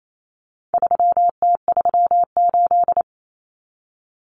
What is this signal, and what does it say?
Morse code, heard as a keyed tone, about 28 wpm, 700 Hz. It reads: 3T38
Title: Common Cut Numbers - 4-Digit 28wpm